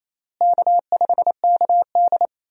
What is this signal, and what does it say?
Morse code, heard as a keyed tone, about 28 wpm, 700 Hz. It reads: K5KD
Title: USA Call Signs - 4 Characters - Repeated 2x - 28wpm